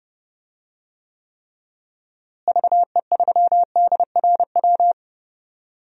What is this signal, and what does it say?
Morse code, heard as a keyed tone, about 30 wpm, 700 Hz. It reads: VE3DRW
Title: DXCC-100 Call Signs - 6 Characters - Repeated 3x - 30wpm